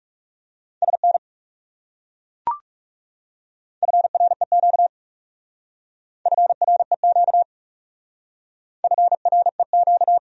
Morse code, heard as a keyed tone, about 40 wpm, 700 Hz. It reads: SN FREQ FREQ FREQ